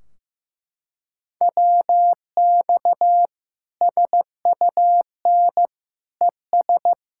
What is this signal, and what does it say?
Morse code, heard as a keyed tone, about 15 wpm, 700 Hz. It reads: WX SUN ES
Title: Practice QSO - Exchange 2 - Info - 15wpm